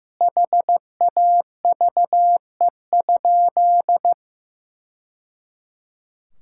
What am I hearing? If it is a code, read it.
Morse code, heard as a keyed tone, about 15 wpm, 700 Hz. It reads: HAVE?
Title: Sentences from Top 1000 Words - Review Part 5 - 15wpm